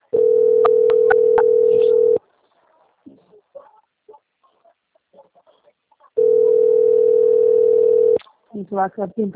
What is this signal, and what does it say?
Ringback tone (North America), 440 + 480 Hz, with speech.
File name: out- - - - - .1322